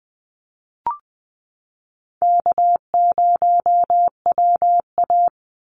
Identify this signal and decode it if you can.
Morse code, heard as a keyed tone, about 20 wpm, 700 Hz. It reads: K0WA
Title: Call Signs Repeated 3x - 20wpm